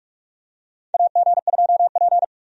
Morse code, heard as a keyed tone, about 46 wpm, 700 Hz. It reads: AG2P
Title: USA Call Signs - 4 Characters - 45wpm